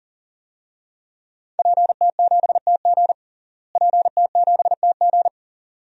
Morse code, heard as a keyed tone, about 37 wpm, 700 Hz. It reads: PT7TG PT7TG
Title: DXCC-100 Call Signs - 5 Characters - Repeated 2x - 40wpm